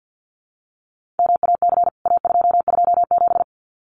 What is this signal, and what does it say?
Morse code, heard as a keyed tone, about 48 wpm, 700 Hz. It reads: MAC A227
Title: CWT Exchange 50wpm